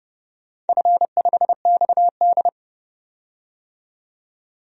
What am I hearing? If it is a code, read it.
Morse code, heard as a keyed tone, about 31 wpm, 700 Hz. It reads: F5XD
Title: DXCC-100 Call Signs - 4 Characters - 30wpm